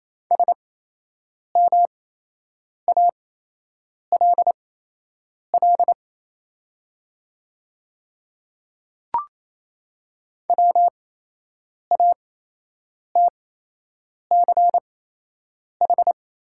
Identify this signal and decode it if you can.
Morse code, heard as a keyed tone, about 27 wpm, spaced as for 8 wpm, 700 Hz. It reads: SMALL WATCH
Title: 8x Character Spacing - Top 500 Words - 28wpm